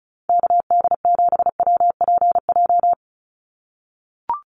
Morse code, heard as a keyed tone, about 34 wpm, 700 Hz. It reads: KD7WPJ
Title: NA SOTA Activator Callsigns 35wpm